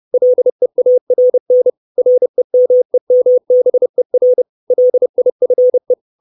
Morse code, heard as a keyed tone, about 30 wpm, 500 Hz. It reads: LEARN REMEMBER LIFE
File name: 632 Words 30WPM